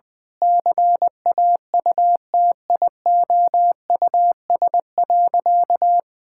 Morse code, heard as a keyed tone, about 19 wpm, 700 Hz. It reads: CAUTIOUS.
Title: Sentences with 8-Word Phrases - 20wpm